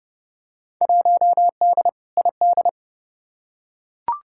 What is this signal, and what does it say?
Morse code, heard as a keyed tone, about 30 wpm, 700 Hz. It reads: 1D ID
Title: Field Day Exchanges - 30wpm